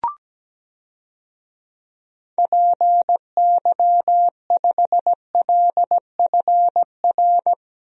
Morse code, heard as a keyed tone, about 17 wpm, 700 Hz. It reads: PY5LFR
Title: DXCC-100 Call Signs - 6 Characters - Repeated 2x - 17wpm